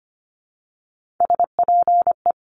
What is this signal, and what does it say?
Morse code, heard as a keyed tone, about 25 wpm, 700 Hz. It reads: SPE
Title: Top Three Letter Combinations in Words 25wpm